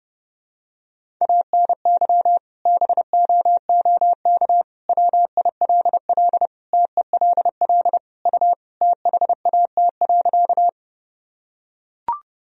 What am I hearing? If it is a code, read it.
Morse code, heard as a keyed tone, about 30 wpm, 700 Hz. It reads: ANY BOOK WILL TELL U THAT.
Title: Sentences from Top 300 Words and Abbreviations 30wpm